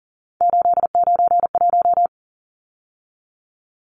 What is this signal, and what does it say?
Morse code, heard as a keyed tone, about 40 wpm, 700 Hz. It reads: 891
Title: Numbers - 3-Digit 40wpm